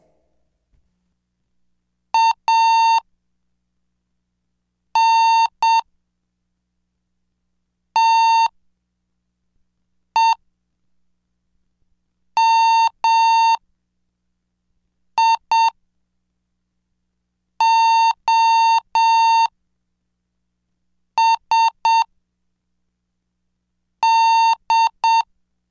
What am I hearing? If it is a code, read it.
Morse code, heard as a keyed tone, about 7 wpm, 910 Hz. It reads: A N T E M I O S D